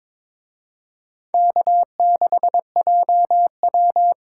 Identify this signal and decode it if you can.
Morse code, heard as a keyed tone, about 26 wpm, 700 Hz. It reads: K6JW
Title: Speed Racing - USA Call Signs 22wpm